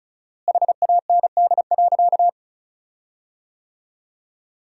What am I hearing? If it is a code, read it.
Morse code, heard as a keyed tone, about 36 wpm, 700 Hz. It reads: HAND.
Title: 4x Word-Spacing - Sentences from Top 400 Words - 35wpm